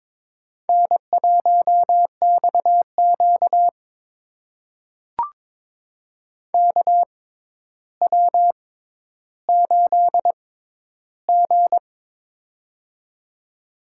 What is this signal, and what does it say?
Morse code, heard as a keyed tone, about 22 wpm, spaced as for 11 wpm, 700 Hz. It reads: N1XQ KW8G
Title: USA Call Signs - 4 Characters - 6x Character Spacing - 22wpm